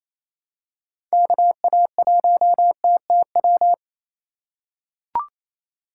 Morse code, heard as a keyed tone, about 28 wpm, 700 Hz. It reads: KA1TTW
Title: USA Call Signs - 6 Characters - 2x Character Spacing - 28wpm